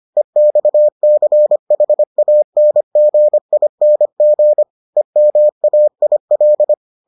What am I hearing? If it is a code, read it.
Morse code, heard as a keyed tone, about 25 wpm, 600 Hz. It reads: EXCHANGING EMAIL